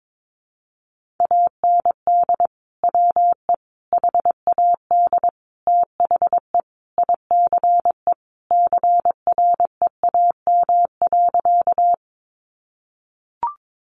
Morse code, heard as a keyed tone, about 22 wpm, 700 Hz. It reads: AND WE HAD THE ICE CREAM.
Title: Sentences from Top 1500 Words 22wpm